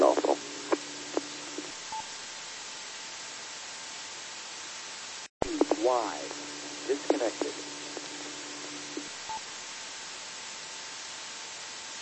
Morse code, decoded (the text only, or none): none